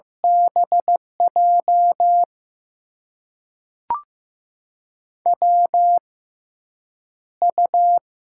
Morse code, heard as a keyed tone, about 15 wpm, spaced as for 5 wpm, 700 Hz. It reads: BJWU
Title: USA Call Signs - 6x Character Spacing - 15wpm